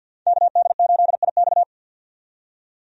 Morse code, heard as a keyed tone, about 45 wpm, 700 Hz. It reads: KD8IX
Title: USA Call Signs - 5 Characters - 50wpm